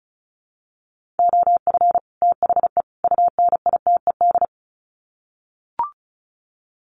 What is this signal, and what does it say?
Morse code, heard as a keyed tone, about 34 wpm, 700 Hz. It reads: OF THE UNITED
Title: Sets of 3 Words 35wpm